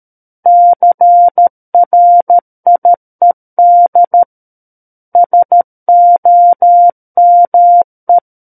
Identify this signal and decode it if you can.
Morse code, heard as a keyed tone, about 13 wpm, 700 Hz. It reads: CRIED SOME
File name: Book One / 04-Chapter 4 - 13 WPM